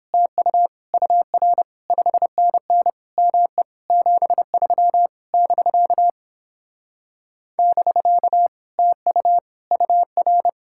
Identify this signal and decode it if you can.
Morse code, heard as a keyed tone, about 29 wpm, 700 Hz. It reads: TU UR 5NN ME 73 <BK> <BK> TU UR